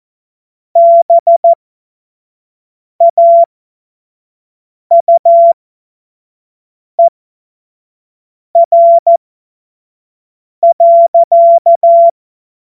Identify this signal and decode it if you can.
Morse code, heard as a keyed tone, about 14 wpm, spaced as for 5 wpm, 680 Hz. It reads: BAUER.
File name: Witz-16-September-2025-25-F70